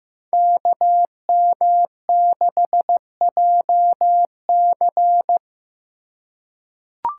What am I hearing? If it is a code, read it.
Morse code, heard as a keyed tone, about 15 wpm, 700 Hz. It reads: KM6JC
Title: Code Groups - 5 Letter:Numbers - 15wpm